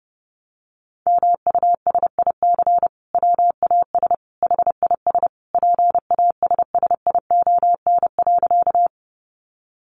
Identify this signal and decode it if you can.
Morse code, heard as a keyed tone, about 30 wpm, 700 Hz. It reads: MUSIC WAS HIS PASSION.
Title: Sentences from Top 1300 Words 30wpm